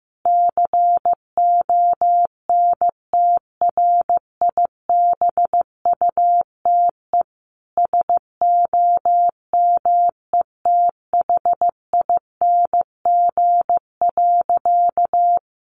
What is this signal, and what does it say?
Morse code, heard as a keyed tone, about 15 wpm, 700 Hz. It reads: CONTRIBUTE SOMETHING.
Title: Sentences from Top 1400 Words 15wpm